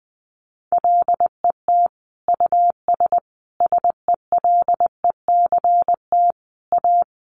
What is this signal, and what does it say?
Morse code, heard as a keyed tone, about 20 wpm, 700 Hz. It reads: LET US SELECT A
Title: Sentences from Top 1300 Words 20wpm